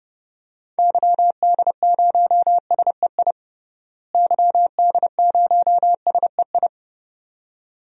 Morse code, heard as a keyed tone, about 31 wpm, 700 Hz. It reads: YD0SEI YD0SEI
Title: DXCC-100 Call Signs - 6 Characters - Repeated 2x - 30wpm